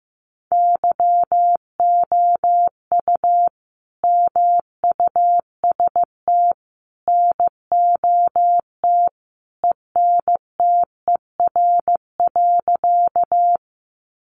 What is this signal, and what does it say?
Morse code, heard as a keyed tone, about 15 wpm, 700 Hz. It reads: YOU MUST NOT ENTER.
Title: Sentences from Top 1300 Words 15wpm